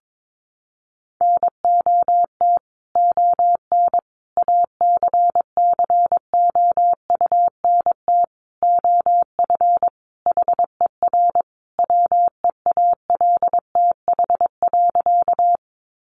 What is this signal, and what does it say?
Morse code, heard as a keyed tone, about 22 wpm, 700 Hz. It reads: NOT ON ACCOUNT OF HER WEALTH.
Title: Sentences from Top 1200 Words 22wpm